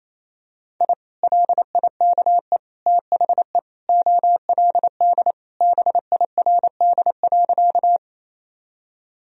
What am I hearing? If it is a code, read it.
Morse code, heard as a keyed tone, about 28 wpm, 700 Hz. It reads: I LIKE THE OLD BIRD.
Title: Sentences from Top 1000 Words - Review Part 2 - 28wpm